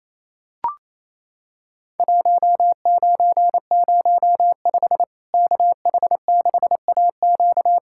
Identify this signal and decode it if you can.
Morse code, heard as a keyed tone, about 28 wpm, 700 Hz. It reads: 1905 KH6AQ
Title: Worked All Europe DX Contest - QTC Exchange - 28wpm